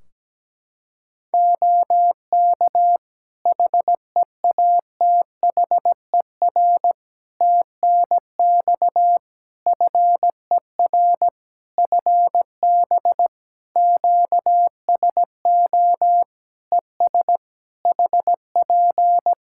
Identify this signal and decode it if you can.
Morse code, heard as a keyed tone, about 17 wpm, 700 Hz. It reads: OK HEATHER TNX FER FB QSO ES HP
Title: K1USN SST QSO Practice 17wpm